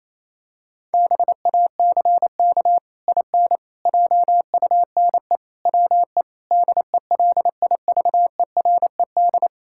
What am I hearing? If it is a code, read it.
Morse code, heard as a keyed tone, about 28 wpm, 700 Hz. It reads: BACK IN JUNE WE DELIVERED
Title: Pangram Challenge v2 - 28wpm